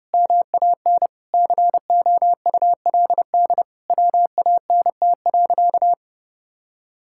Morse code, heard as a keyed tone, about 32 wpm, 700 Hz. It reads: MAN COULD WANT.
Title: Sentences from Top 1000 Words - Review Part 5 - 30wpm